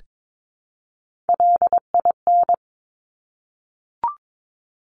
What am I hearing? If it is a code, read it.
Morse code, heard as a keyed tone, about 22 wpm, 700 Hz. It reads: LIN
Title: Top Three Letter Combinations in Words 22wpm